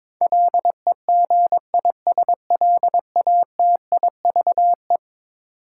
Morse code, heard as a keyed tone, about 22 wpm, 700 Hz. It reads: LEGISLATIVE
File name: practice / 22wpm / Eleven Letter Words - 22wpm